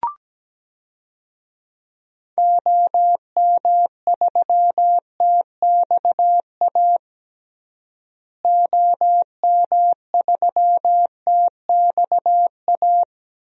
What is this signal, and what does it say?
Morse code, heard as a keyed tone, about 17 wpm, 700 Hz. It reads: OM3TXA OM3TXA
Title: DXCC-100 Call Signs - 6 Characters - Repeated 2x - 17wpm